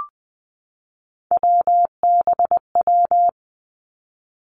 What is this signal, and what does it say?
Morse code, heard as a keyed tone, about 20 wpm, 700 Hz. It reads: WBW
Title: Three Challenging Letter-Number Combinations 20wpm